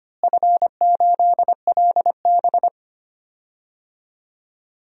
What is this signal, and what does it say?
Morse code, heard as a keyed tone, about 25 wpm, 700 Hz. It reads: F8LB
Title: DXCC-100 Call Signs - 4 Characters - Repeated 3x - 25wpm